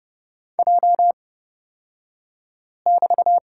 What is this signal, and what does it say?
Morse code, heard as a keyed tone, about 30 wpm, spaced as for 5 wpm, 700 Hz. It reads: J<BT>
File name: Lesson 38d - Letter q - Two Characters 30wpm